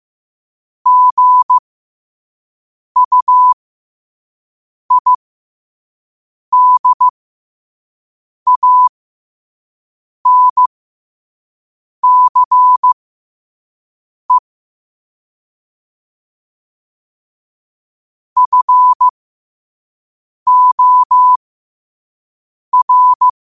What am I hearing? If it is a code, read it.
Morse code, heard as a keyed tone, about 15 wpm, spaced as for 5 wpm, 1000 Hz. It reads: GUIDANCE FOR
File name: qotdfiles / 5wpm